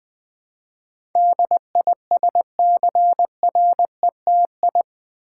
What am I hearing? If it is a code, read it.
Morse code, heard as a keyed tone, about 20 wpm, 700 Hz. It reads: DISCRETI
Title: Build A Word - 4 Letters At A Time - 10-Letter Words - 20wpm